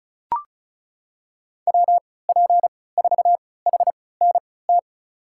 Morse code, heard as a keyed tone, about 36 wpm, spaced as for 20 wpm, 700 Hz. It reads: WP4HNT
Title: USA Call Signs - 6 Characters - 3x Character Spacing - 35wpm